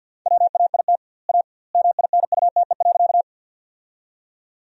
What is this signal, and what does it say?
Morse code, heard as a keyed tone, about 47 wpm, 700 Hz. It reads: WAIT A MINUTE.
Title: 2x Word-Spacing - Sentences from Top 500 Words - 50wpm